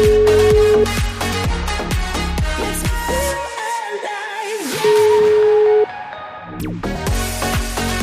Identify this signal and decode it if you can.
Ringback tone (Europe, 425 Hz), with noise.